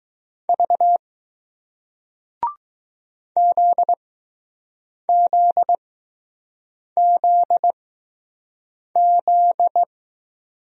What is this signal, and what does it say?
Morse code, heard as a keyed tone, about 20 wpm, spaced as for 6 wpm, 700 Hz. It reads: V ZZZZ